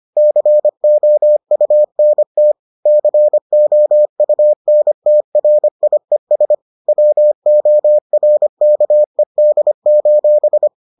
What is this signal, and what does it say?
Morse code, heard as a keyed tone, about 25 wpm, 600 Hz. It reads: COUNT COUNTRIES WORKED: